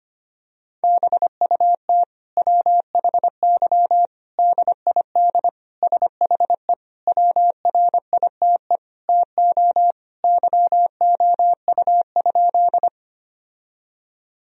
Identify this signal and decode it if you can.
Morse code, heard as a keyed tone, about 25 wpm, 700 Hz. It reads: BUT WHY DID SHE WRITE TO YOU?
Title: Sentences from Top 1000 Words - Review Part 1 - 25wpm